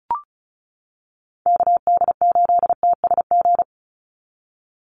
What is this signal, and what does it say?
Morse code, heard as a keyed tone, about 34 wpm, 700 Hz. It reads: KD8TSG
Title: USA Call Signs 35wpm